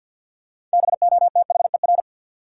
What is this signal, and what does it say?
Morse code, heard as a keyed tone, about 48 wpm, 700 Hz. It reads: BOTHER
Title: Six Letter Words - 50wpm